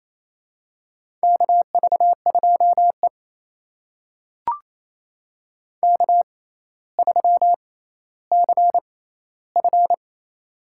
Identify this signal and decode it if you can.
Morse code, heard as a keyed tone, about 28 wpm, spaced as for 14 wpm, 700 Hz. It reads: KV2E K3CF
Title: USA Call Signs - 4 Characters - 6x Character Spacing - 28wpm